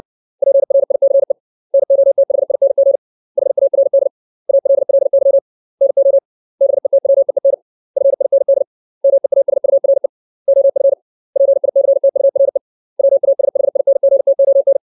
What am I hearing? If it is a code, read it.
Morse code, heard as a keyed tone, about 60 wpm, 550 Hz. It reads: PRICE NOTHING HARD ALLY NO BETWEEN FIND MEASURE OF PICTURE WASHINGTON
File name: 632 words / 632 Words 60WPM